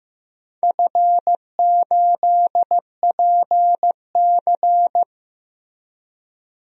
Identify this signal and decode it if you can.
Morse code, heard as a keyed tone, about 15 wpm, 700 Hz. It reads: F8PC
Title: DXCC-100 Call Signs - 4 Characters - Repeated 3x - 15wpm